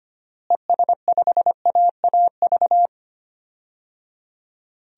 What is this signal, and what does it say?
Morse code, heard as a keyed tone, about 25 wpm, 700 Hz. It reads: ES5AAV
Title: DXCC-100 Call Signs - 6 Characters - Repeated 3x - 25wpm